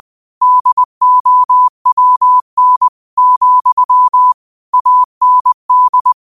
Morse code, heard as a keyed tone, about 20 wpm, 1000 Hz. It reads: DOWN, AND
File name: qotdfiles / 20wpm